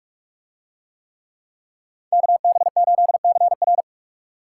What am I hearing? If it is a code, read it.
Morse code, heard as a keyed tone, about 46 wpm, 700 Hz. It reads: KB8CR